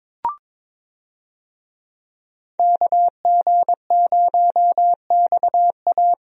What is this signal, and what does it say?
Morse code, heard as a keyed tone, about 21 wpm, 700 Hz. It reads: KG0XA
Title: USA Call Signs - 5 Characters - 22wpm